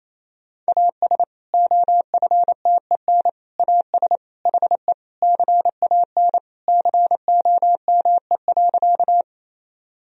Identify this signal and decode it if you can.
Morse code, heard as a keyed tone, about 28 wpm, 700 Hz. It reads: AS OFTEN AS HE CAN COME.
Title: Sentences from Top 1000 Words - Review Part 1 - 28wpm